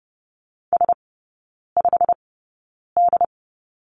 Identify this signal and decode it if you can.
Morse code, heard as a keyed tone, about 30 wpm, spaced as for 9 wpm, 700 Hz. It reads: S5D
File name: course / 30:10wpm / 087 Lesson 15e - Number 5 - Three Characters 30wpm